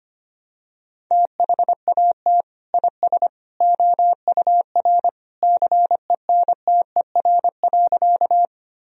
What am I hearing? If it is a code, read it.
Morse code, heard as a keyed tone, about 25 wpm, 700 Hz. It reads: THAT IS OUR CENTER.
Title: Sentences from Top 1000 Words - Review Part 2 - 25wpm